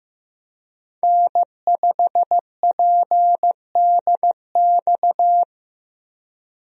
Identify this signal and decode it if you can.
Morse code, heard as a keyed tone, about 15 wpm, 700 Hz. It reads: N5PDX
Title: USA Call Signs - 5 Characters - Repeated 3x - 15wpm